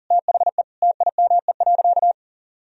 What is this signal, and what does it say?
Morse code, heard as a keyed tone, about 40 wpm, 700 Hz. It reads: THE TIME.
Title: Sentences from Top 1000 Words - Review Part 3 - 40wpm